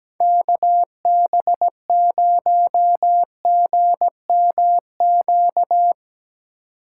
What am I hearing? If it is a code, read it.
Morse code, heard as a keyed tone, about 17 wpm, 700 Hz. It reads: KB0GMQ
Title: USA Call Signs - 6 Characters - 3x Character Spacing - 17wpm